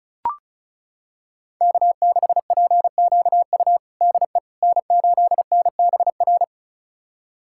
Morse code, heard as a keyed tone, about 36 wpm, 700 Hz. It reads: K6PQU DE N8NBR
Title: USA Callsign DE Callsign - 5 Characters - 35wpm